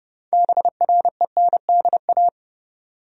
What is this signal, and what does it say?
Morse code, heard as a keyed tone, about 30 wpm, 700 Hz. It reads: BRENDA
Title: USA Common Names - 30wpm